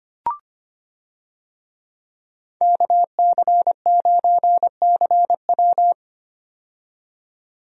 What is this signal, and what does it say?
Morse code, heard as a keyed tone, about 25 wpm, 700 Hz. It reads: KC9CW
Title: USA Call Signs - 5 Characters - 25wpm